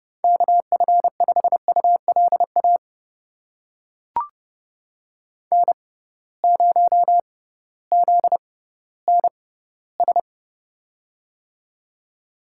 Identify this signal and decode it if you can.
Morse code, heard as a keyed tone, about 30 wpm, spaced as for 16 wpm, 700 Hz. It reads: KF5ULA N0ZNS
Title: USA Call Signs - 6x Character Spacing - 30wpm